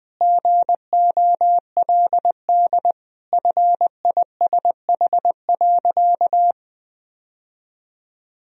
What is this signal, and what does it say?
Morse code, heard as a keyed tone, about 20 wpm, 700 Hz. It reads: GOLD FISH.
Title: Sentences from Top 1000 Words - Review Part 3 - 20wpm